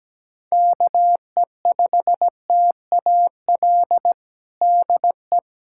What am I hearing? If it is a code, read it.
Morse code, heard as a keyed tone, about 17 wpm, 700 Hz. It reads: KE5TAL DE
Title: USA Callsign DE Callsign - 6 Characters - 17wpm